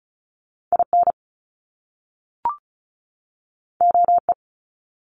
Morse code, heard as a keyed tone, about 36 wpm, 700 Hz. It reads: IN OE
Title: Top Two Letter Combinations in Words 35wpm